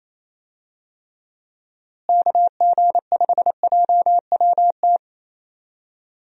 Morse code, heard as a keyed tone, about 28 wpm, 700 Hz. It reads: KG5JWT
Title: USA Call Signs - 6 Characters - Repeated 3x - 28wpm